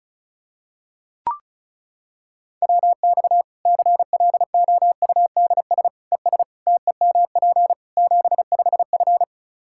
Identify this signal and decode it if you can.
Morse code, heard as a keyed tone, about 36 wpm, 700 Hz. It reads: WX CLOUDS ES TEMP 75F
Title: K1USN SST QSO Practice 35wpm